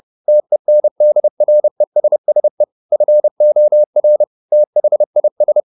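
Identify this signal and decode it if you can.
Morse code, heard as a keyed tone, about 30 wpm, 600 Hz. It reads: TENDRESSE FOR THIS
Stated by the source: Morse code at 30 wpm, the novel Never by Ken Follett in English, from CW med Gnister.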